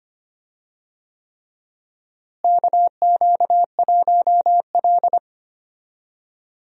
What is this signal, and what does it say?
Morse code, heard as a keyed tone, about 25 wpm, 700 Hz. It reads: KQ1L
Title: USA Call Signs - 4 Characters - Repeated 3x - 25wpm